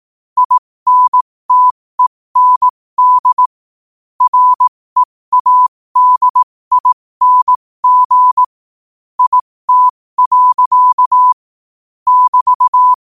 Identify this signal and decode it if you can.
Morse code, heard as a keyed tone, about 18 wpm, spaced as for 15 wpm, 1000 Hz. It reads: INTEND READING IT. <BT>
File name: qotdfiles / 15wpm